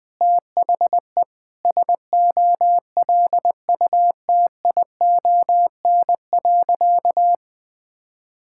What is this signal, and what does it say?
Morse code, heard as a keyed tone, about 20 wpm, 700 Hz. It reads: THE SOLUTION.
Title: Sentences with 8-Word Phrases - 20wpm